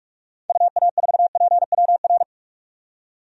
Morse code, heard as a keyed tone, about 45 wpm, 700 Hz. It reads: UA4PWR
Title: DXCC-100 Call Signs - 6 Characters - 45wpm